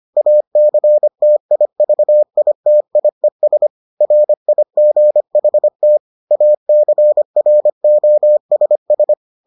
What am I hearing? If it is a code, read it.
Morse code, heard as a keyed tone, about 25 wpm, 600 Hz. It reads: ACTIVITIES RIGHT ACROSS